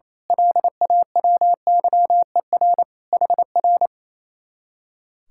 Morse code, heard as a keyed tone, about 26 wpm, 700 Hz. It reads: LAWYER HR
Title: Speed-Racing Common QSO Phrases Encore - 28wpm